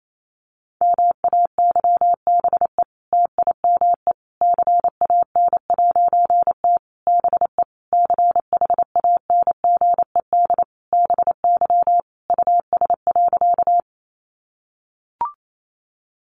Morse code, heard as a keyed tone, about 28 wpm, 700 Hz. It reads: MAYBE TIME CAN'T BE CHANGED BY US.
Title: A Sound Of Thunder 28wpm